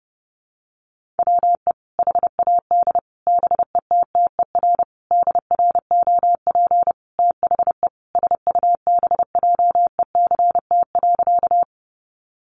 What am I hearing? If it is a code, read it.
Morse code, heard as a keyed tone, about 30 wpm, 700 Hz. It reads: WE HAD BETTER DROP THE SUBJECT.
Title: Sentences from Top 1400 Words 30wpm